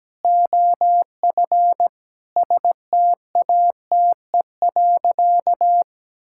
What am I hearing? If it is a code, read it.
Morse code, heard as a keyed tone, about 17 wpm, 700 Hz. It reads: OF STATE.
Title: Sentences from Top 1000 Words - Review Part 5 - 17wpm